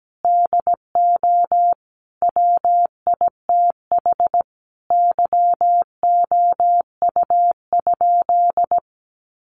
Morse code, heard as a keyed tone, about 17 wpm, 700 Hz. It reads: DO WITH YOU?
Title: Sentences from Top 100 Words 17wpm